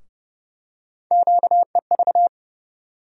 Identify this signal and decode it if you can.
Morse code, heard as a keyed tone, about 30 wpm, 700 Hz. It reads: QEV
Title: USA Call Sign Postfixes - 3 Letters 30wpm